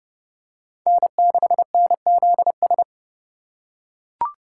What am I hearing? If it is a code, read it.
Morse code, heard as a keyed tone, about 30 wpm, 690 Hz. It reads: N6NZS
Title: USA Call Signs - 5 Characters - 2x Character Spacing - 30wpm